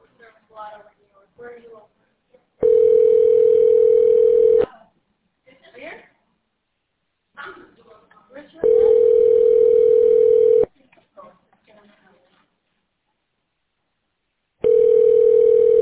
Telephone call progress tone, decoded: ringback tone

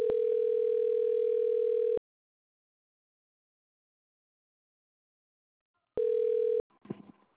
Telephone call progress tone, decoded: ringback tone

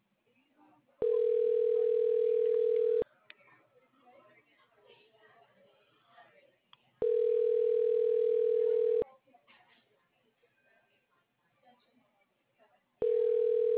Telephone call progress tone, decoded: ringback tone

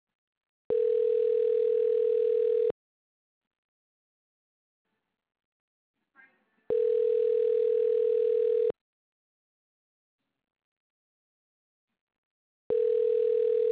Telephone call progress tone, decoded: ringback tone

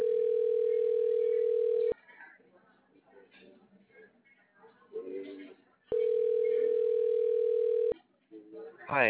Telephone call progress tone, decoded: ringback tone

